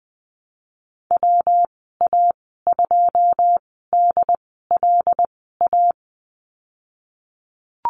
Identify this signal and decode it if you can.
Morse code, heard as a keyed tone, about 20 wpm, spaced as for 15 wpm, 700 Hz. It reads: WA2DLA
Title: 154 Lesson 25g - Letter p - Call Signs 20wpm